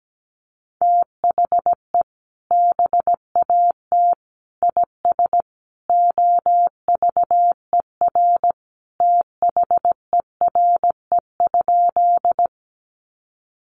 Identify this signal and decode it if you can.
Morse code, heard as a keyed tone, about 17 wpm, 700 Hz. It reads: THE BAT IS OVER THERE?
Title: Sentences from Top 1500 Words 17wpm